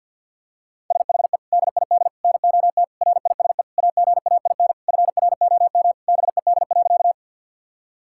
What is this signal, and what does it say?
Morse code, heard as a keyed tone, about 50 wpm, 700 Hz. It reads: SHE DID NOT RISE AGAIN FROM BED.
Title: Sentences from Top 1000 Words - Review Part 6 - 50wpm